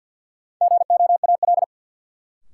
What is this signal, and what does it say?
Morse code, heard as a keyed tone, about 52 wpm, 700 Hz. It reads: GOAL
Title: Sentences from Top 1000 Words - Review Part 5 - New Word Drill - 50wpm